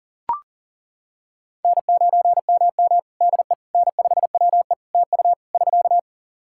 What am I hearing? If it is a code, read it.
Morse code, heard as a keyed tone, about 40 wpm, 700 Hz. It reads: N9MM DE N5WE TU <SK>
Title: Practice QSO - Exchange 4 - Ending - 40wpm